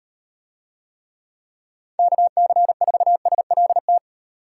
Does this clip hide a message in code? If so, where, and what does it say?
Yes: Morse code from 2.0 to 4.0 s, heard as a keyed tone, about 38 wpm, 700 Hz. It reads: KC4SLT